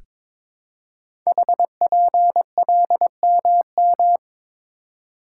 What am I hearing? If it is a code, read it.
Morse code, heard as a keyed tone, about 23 wpm, 700 Hz. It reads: HPLMM
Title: Code Groups - 5 Letters - 22wpm